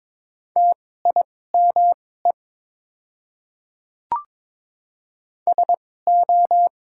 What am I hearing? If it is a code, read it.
Morse code, heard as a keyed tone, about 22 wpm, spaced as for 16 wpm, 690 Hz. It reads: TIME SO
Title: 2x Character Spacing - Top 100 Words - 22wpm